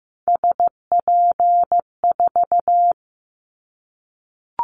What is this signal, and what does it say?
Morse code, heard as a keyed tone, about 15 wpm, 700 Hz. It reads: SP4
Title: Callsign Prefixes for Top 25 DXCC - 15wpm